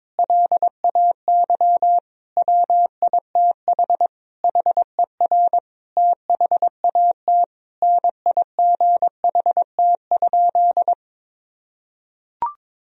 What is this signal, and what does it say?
Morse code, heard as a keyed tone, about 22 wpm, 700 Hz. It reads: LAY WITH HER THAT NIGHT?
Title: Sentences from Top 1000 Words - Review Part 2 - 22wpm